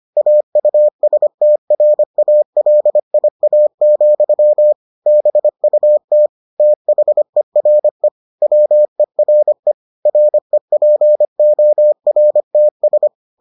Morse code, heard as a keyed tone, about 25 wpm, 600 Hz. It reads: AUSTRALIA, BUT THERE WERE REPORTS